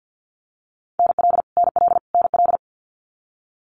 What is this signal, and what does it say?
Morse code, heard as a keyed tone, about 50 wpm, 700 Hz. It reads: NL NL NL
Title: Canadian Provinces and Territories - Repeated 3x - 50wpm